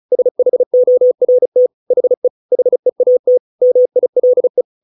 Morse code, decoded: SHORT HE HEAT MILE

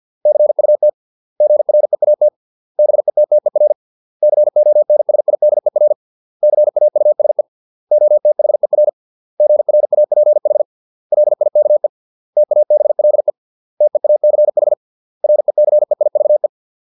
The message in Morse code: CUT GREAT BETTER CONSIDER CAUSE OTHER GRAPH LIKE TABLE TEACH RECEIVE

